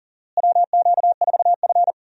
Morse code, heard as a keyed tone, about 40 wpm, 700 Hz. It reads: WQ4F
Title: USA Call Signs - 4 Characters - Repeated 2x - 40wpm